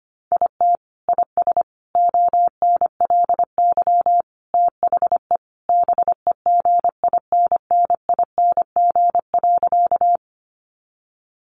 Morse code, heard as a keyed tone, about 25 wpm, 700 Hz. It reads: IT IS ONLY THE BEGINNING.
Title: Sentences from Top 800 Words 25wpm